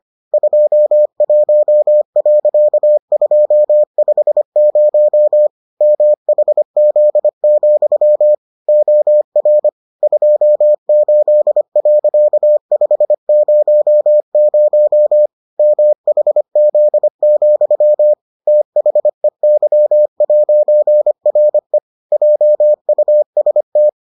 Morse code, heard as a keyed tone, about 25 wpm, 600 Hz. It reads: 21.250 MHZ, OR 28.500 MHZ, THEY'RE JUST